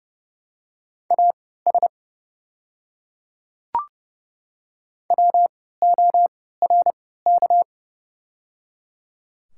Morse code, heard as a keyed tone, about 30 wpm, spaced as for 17 wpm, 690 Hz. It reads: AS WORK
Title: 3x Character Spacing - Top 200 Words - 30wpm